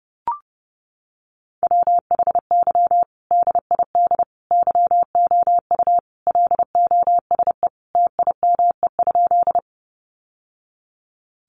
Morse code, heard as a keyed tone, about 30 wpm, 700 Hz. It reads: WHY DID YOU LOSE TIME?
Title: Sentences from Top 1100 Words 30wpm